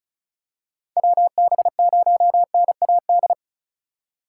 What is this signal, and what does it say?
Morse code, heard as a keyed tone, about 36 wpm, 700 Hz. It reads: WB0NAD
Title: USA Call Signs - 6 Characters - 4x Character Spacing - 35wpm